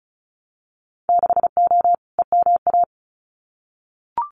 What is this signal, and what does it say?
Morse code, heard as a keyed tone, about 36 wpm, 700 Hz. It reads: 6O EMA
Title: Winter Field Day Exchanges Encore 35wpm